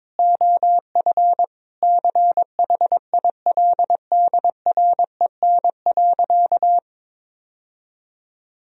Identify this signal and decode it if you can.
Morse code, heard as a keyed tone, about 22 wpm, 700 Hz. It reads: OF CHILDREN.
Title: Sentences from Top 1000 Words - Review Part 1 - 22wpm